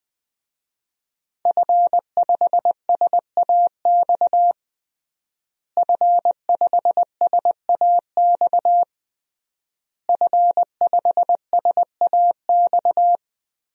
Morse code, heard as a keyed tone, about 20 wpm, 700 Hz. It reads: F5SAX F5SAX F5SAX
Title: DXCC-100 Call Signs - 5 Characters - Repeated 3x - 20wpm